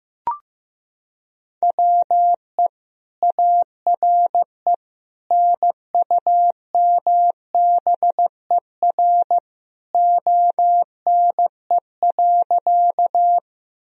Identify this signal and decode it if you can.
Morse code, heard as a keyed tone, about 15 wpm, 700 Hz. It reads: WE ARE NUMBER ONE.
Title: Sentences from Top 1000 Words - Review Part 1 - 15wpm